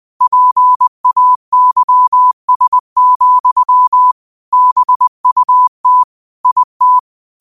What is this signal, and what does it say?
Morse code, heard as a keyed tone, about 20 wpm, 1000 Hz. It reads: PAYS, BUT IT
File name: qotdfiles / 20wpm